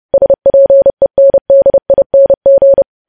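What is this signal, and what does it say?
Morse code, heard as a keyed tone, about 30 wpm, 550 Hz. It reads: SPENDING
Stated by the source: News headlines sent in Morse at 30 words per minute.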